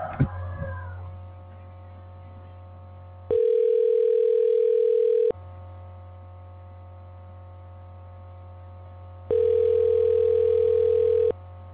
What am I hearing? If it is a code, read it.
Ringback tone (North America), 440 + 480 Hz.